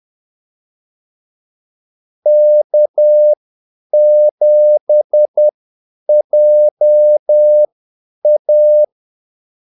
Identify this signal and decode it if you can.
Morse code, heard as a keyed tone, about 10 wpm, spaced as for 8 wpm, 600 Hz. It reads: K7JA
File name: Callsigns-10-8wpm